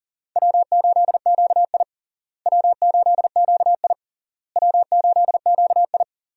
Morse code, heard as a keyed tone, about 37 wpm, 700 Hz. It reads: W8QI W8QI W8QI
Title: USA Call Signs - 4 Characters - Repeated 3x - 40wpm